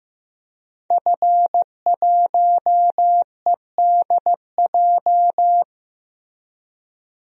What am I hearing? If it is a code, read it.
Morse code, heard as a keyed tone, about 15 wpm, 700 Hz. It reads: F1EDJ
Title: DXCC-100 Call Signs - 5 Characters - 15wpm